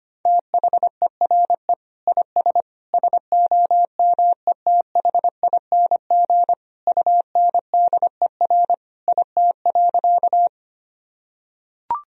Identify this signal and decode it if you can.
Morse code, heard as a keyed tone, about 25 wpm, 700 Hz. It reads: THERE IS SOMETHING UNDER IT.
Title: Sentences from Top 1000 Words - Review Part 1 - 25wpm